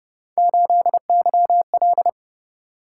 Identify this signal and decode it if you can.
Morse code, heard as a keyed tone, about 29 wpm, 700 Hz. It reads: 8YL